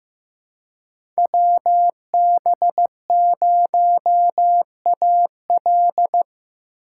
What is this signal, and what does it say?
Morse code, heard as a keyed tone, about 15 wpm, 700 Hz. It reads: WB0AL
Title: USA Call Signs - 5 Characters - 2x Character Spacing - 15wpm